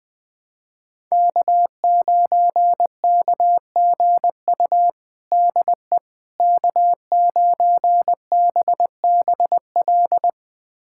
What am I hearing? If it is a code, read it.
Morse code, heard as a keyed tone, about 20 wpm, 700 Hz. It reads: K9KGU DE K9BBL
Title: USA Callsign DE Callsign - 5 Characters - 20wpm